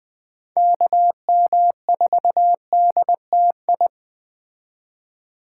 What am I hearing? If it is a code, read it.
Morse code, heard as a keyed tone, about 20 wpm, 700 Hz. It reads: KM4DTI